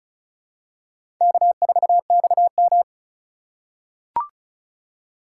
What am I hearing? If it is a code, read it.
Morse code, heard as a keyed tone, about 36 wpm, 700 Hz. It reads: K4XM
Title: USA Call Signs - 4x Character Spacing - 35wpm